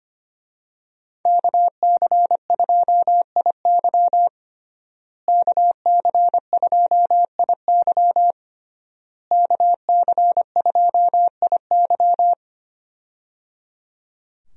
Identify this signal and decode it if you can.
Morse code, heard as a keyed tone, about 25 wpm, 700 Hz. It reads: KC2IY KC2IY KC2IY
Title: USA Call Signs - 5 Characters - Repeated 3x - 25wpm